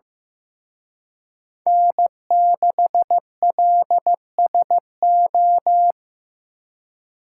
Morse code, heard as a keyed tone, about 18 wpm, 700 Hz. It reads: N6LSO
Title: Speed Racing - USA 1x3 Call Signs 15wpm